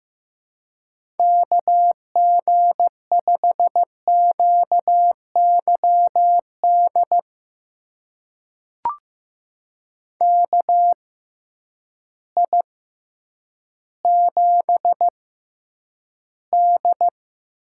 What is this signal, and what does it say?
Morse code, heard as a keyed tone, about 15 wpm, spaced as for 9 wpm, 700 Hz. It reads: KG5QYD KI7D
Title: USA Call Signs - 6 Characters - 6x Character Spacing - 15wpm